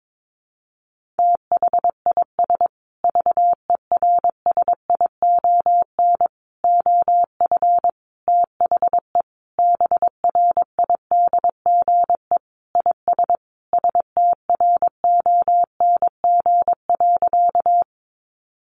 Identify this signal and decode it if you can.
Morse code, heard as a keyed tone, about 22 wpm, 700 Hz. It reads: THIS VERSION OF THE BRIDGE IS STRONG.
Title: Sentences from Top 1400 Words 22wpm